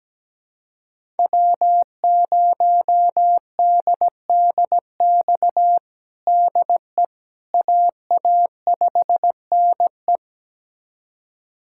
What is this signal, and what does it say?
Morse code, heard as a keyed tone, about 17 wpm, 700 Hz. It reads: W0DDX DE AA5NE
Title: USA Callsign DE Callsign - 5 Characters - 17wpm